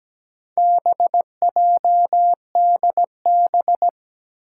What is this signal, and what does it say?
Morse code, heard as a keyed tone, about 17 wpm, 700 Hz. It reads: BJDB
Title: Four Challenging Letter-Number Combinations 17wpm